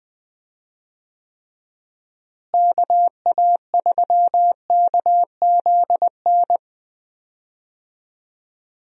Morse code, heard as a keyed tone, about 20 wpm, 700 Hz. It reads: KA3KZN
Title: USA Call Signs - 6 Characters - Repeated 3x - 20wpm